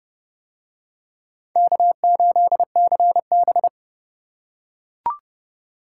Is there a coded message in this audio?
Yes: Morse code from 1.6 to 3.7 s, heard as a keyed tone, about 27 wpm, 700 Hz. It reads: K8CB